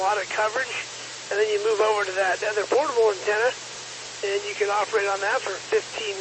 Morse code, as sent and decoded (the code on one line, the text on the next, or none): none